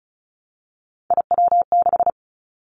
Morse code, heard as a keyed tone, about 36 wpm, 700 Hz. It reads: IW6
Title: Three Challenging Letter-Number Combinations 35wpm